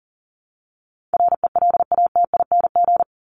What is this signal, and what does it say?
Morse code, heard as a keyed tone, about 39 wpm, 700 Hz. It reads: RELATING
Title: ING Suffix - Top 100 Words with Base Words 40wpm